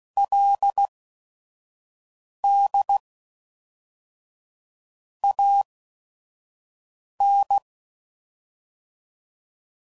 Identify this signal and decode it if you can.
Morse code, heard as a keyed tone, about 16 wpm, spaced as for 5 wpm, 800 Hz. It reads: LD AN
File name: CW Code Groups @5WPM / QSOs / kc7haa